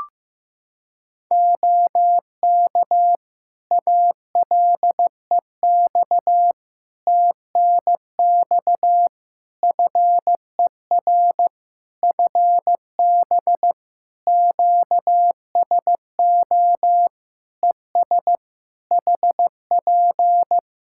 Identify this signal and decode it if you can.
Morse code, heard as a keyed tone, about 15 wpm, 700 Hz. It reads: OK ALEX TNX FER FB QSO ES HP
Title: Practice QSO - Exchange 4 - Ending - 15wpm